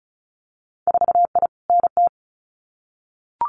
Morse code, heard as a keyed tone, about 36 wpm, 700 Hz. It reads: 4I NT
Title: Winter Field Day Exchanges Encore 35wpm